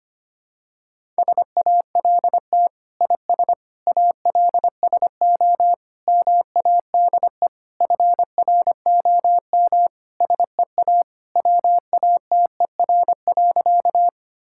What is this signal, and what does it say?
Morse code, heard as a keyed tone, about 25 wpm, 700 Hz. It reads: SALT IS ALSO MADE FROM SEA WATER.
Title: Sentences from Top 1000 Words - Review Part 5 - 25wpm